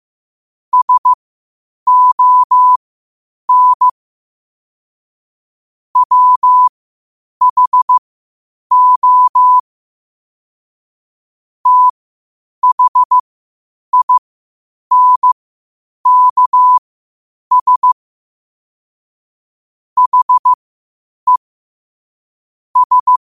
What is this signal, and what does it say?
Morse code, heard as a keyed tone, about 15 wpm, spaced as for 8 wpm, 1000 Hz. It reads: SON WHO THINKS HE S
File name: qotdfiles / 8wpm